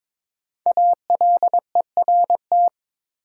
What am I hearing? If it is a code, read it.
Morse code, heard as a keyed tone, about 22 wpm, 690 Hz. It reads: ALERT
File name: 5 Letter Words - First 3 Letters Given - 22wpm